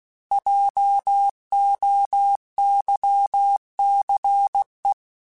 Morse code, heard as a keyed tone, about 16 wpm, 800 Hz. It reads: JOYCE
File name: Alphabetical / Alphabetical mp3s / Js @15